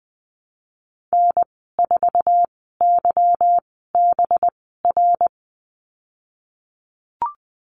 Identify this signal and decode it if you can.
Morse code, heard as a keyed tone, about 19 wpm, spaced as for 14 wpm, 700 Hz. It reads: N4YBR
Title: 186 Lesson 30g - Letter b - Call Signs 20wpm